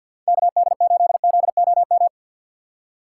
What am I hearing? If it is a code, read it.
Morse code, heard as a keyed tone, about 52 wpm, 700 Hz. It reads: KD8ZOM
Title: USA Call Signs - 6 Characters - 3x Character Spacing - 50wpm